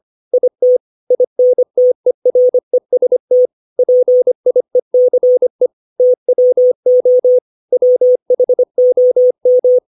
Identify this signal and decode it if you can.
Morse code, heard as a keyed tone, about 25 wpm, 500 Hz. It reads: IT INTEREST PIECE TWO WHOM
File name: 632 words / 632 Words 25WPM